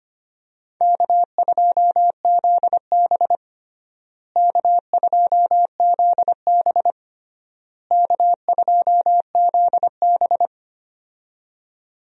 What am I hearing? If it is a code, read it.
Morse code, heard as a keyed tone, about 25 wpm, 700 Hz. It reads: K2ZB K2ZB K2ZB
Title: USA Call Signs - 4 Characters - Repeated 3x - 25wpm